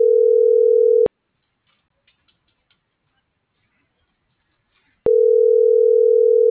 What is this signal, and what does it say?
Ringback tone (North America), 440 + 480 Hz.